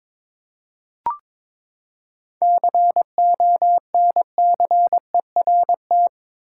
Morse code, heard as a keyed tone, about 22 wpm, 700 Hz. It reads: CONCERT
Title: Seven Letter Words - 22wpm